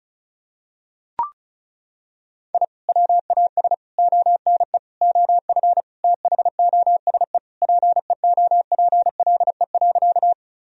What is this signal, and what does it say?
Morse code, heard as a keyed tone, about 36 wpm, 700 Hz. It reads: I WAS ONE OF THOSE PEOPLE.
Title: Sentences from Top 1000 Words - Review Part 1 - 35wpm